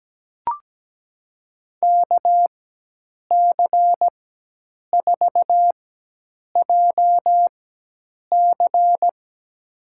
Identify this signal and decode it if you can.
Morse code, heard as a keyed tone, about 17 wpm, spaced as for 8 wpm, 700 Hz. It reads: KC4JC
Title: USA Call Signs - 4x Character Spacing - 17wpm